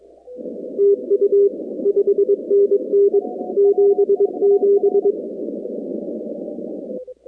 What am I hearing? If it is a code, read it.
Morse code, heard as a keyed tone, about 22 wpm, 400 Hz. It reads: TU 5NN 77
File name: contests / 2019-cw-wpx-kt5j-k5pi- -2330 0-46